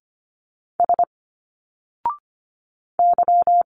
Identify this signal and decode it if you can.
Morse code, heard as a keyed tone, about 25 wpm, 700 Hz. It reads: S Y
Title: Single Letters 25wpm